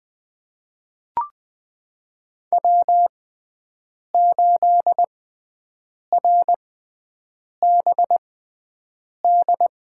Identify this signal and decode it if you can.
Morse code, heard as a keyed tone, about 21 wpm, spaced as for 7 wpm, 700 Hz. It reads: W8RBD
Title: USA Call Signs - 5 Characters - 6x Character Spacing - 20wpm